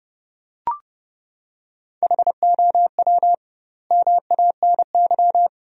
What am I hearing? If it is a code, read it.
Morse code, heard as a keyed tone, about 32 wpm, 700 Hz. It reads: HOW MANY
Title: 2x Word-Spacing - Sentences from Top 400 Words - 30wpm